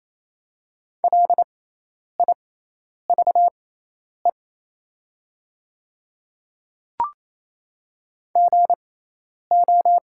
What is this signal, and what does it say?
Morse code, heard as a keyed tone, about 27 wpm, spaced as for 10 wpm, 700 Hz. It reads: LIVE GO
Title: 6x Character Spacing - Top 200 Words - 28wpm